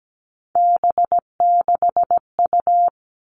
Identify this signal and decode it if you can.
Morse code, heard as a keyed tone, about 17 wpm, 700 Hz. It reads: B6U
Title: Three Challenging Letter-Number Combinations 17wpm